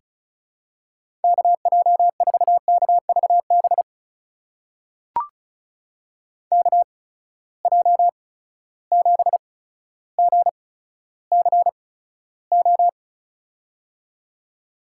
Morse code, heard as a keyed tone, about 36 wpm, spaced as for 15 wpm, 700 Hz. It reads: KJ4KVB KJ7GCO
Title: USA Call Signs - 6 Characters - 8x Character Spacing - 35wpm